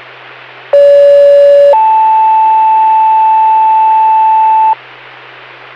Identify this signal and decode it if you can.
Two-tone page, heard as a one-second tone, then a three-second tone, about 570 then 880 Hz, with noise. It is heard from 0.7 to 4.7 s.